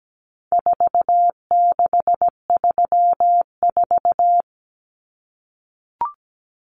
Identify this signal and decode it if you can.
Morse code, heard as a keyed tone, about 17 wpm, 700 Hz. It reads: 4634
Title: Numbers - 4-Digit 17wpm